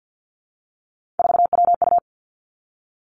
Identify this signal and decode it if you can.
Morse code, heard as a keyed tone, about 50 wpm, 700 Hz. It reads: 4WU
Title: Three Challenging Letter-Number Combinations 50wpm